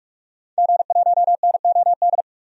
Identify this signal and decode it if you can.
Morse code, heard as a keyed tone, about 46 wpm, 700 Hz. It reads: G1NOD
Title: DXCC-100 Call Signs - 5 Characters - Repeated 2x - 45wpm